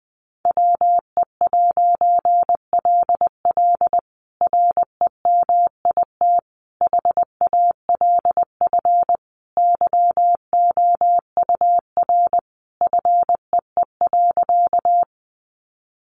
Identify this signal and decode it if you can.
Morse code, heard as a keyed tone, about 20 wpm, 700 Hz. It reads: WE'LL REMIT HALF YOUR FEE.
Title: A Sound Of Thunder 20wpm